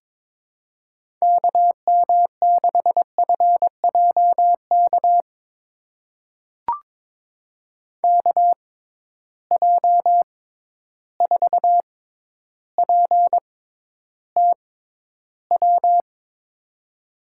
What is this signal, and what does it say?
Morse code, heard as a keyed tone, about 22 wpm, spaced as for 11 wpm, 700 Hz. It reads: KM6FJK KJ4PTW
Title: USA Call Signs - 6x Character Spacing - 22wpm